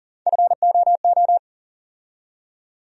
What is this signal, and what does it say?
Morse code, heard as a keyed tone, about 40 wpm, 700 Hz. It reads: FOO
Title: USA Call Sign Postfixes - 3 Letters 40wpm